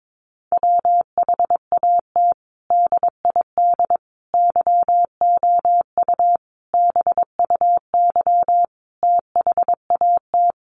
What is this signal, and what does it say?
Morse code, heard as a keyed tone, about 22 wpm, 700 Hz. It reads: WHAT DID YOU BUY THAT
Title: Sentences from Top 700 Words 22wpm